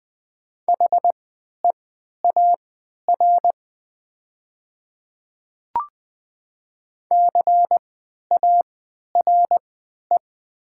Morse code, heard as a keyed tone, about 20 wpm, spaced as for 11 wpm, 700 Hz. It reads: HEAR CARE
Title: 3x Character Spacing - Top 300 Words - 20wpm